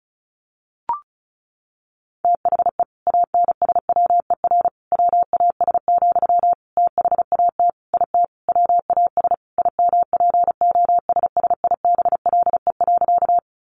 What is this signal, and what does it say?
Morse code, heard as a keyed tone, about 34 wpm, 700 Hz. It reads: THE ANSWER WAS, THAT IT WAS IMPOSSIBLE.
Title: Sentences from Top 1100 Words 35wpm